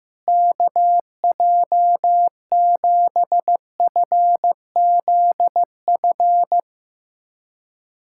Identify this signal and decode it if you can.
Morse code, heard as a keyed tone, about 15 wpm, 700 Hz. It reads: KJ7FZF
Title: USA Callsign DE Callsign 15wpm